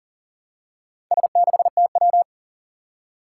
Morse code, heard as a keyed tone, about 40 wpm, 700 Hz. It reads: S6TW
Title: Four Challenging Letter-Number Combinations 40wpm